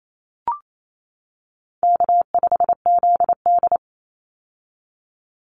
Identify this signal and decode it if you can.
Morse code, heard as a keyed tone, about 28 wpm, 700 Hz. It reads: K5ZD
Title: Easy Call Signs - 28wpm